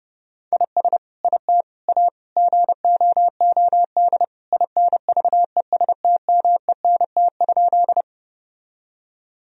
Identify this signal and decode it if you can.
Morse code, heard as a keyed tone, about 30 wpm, 700 Hz. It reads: IS IT A GOOD INVESTMENT?
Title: Sentences from Top 1000 Words - Review Part 4 - 30wpm